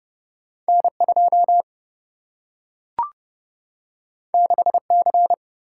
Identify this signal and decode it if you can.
Morse code, heard as a keyed tone, about 29 wpm, 700 Hz. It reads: N2 6C
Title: Single-Letter Single-Number 30wpm